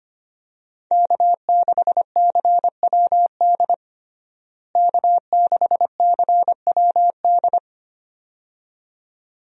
Morse code, heard as a keyed tone, about 25 wpm, 700 Hz. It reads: K6CWD K6CWD
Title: USA Call Signs - 5 Characters - Repeated 2x - 25wpm